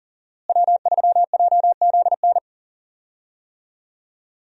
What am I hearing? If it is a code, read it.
Morse code, heard as a keyed tone, about 40 wpm, 700 Hz. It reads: W3JZN
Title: USA Call Signs - 5 Characters - Repeated 2x - 40wpm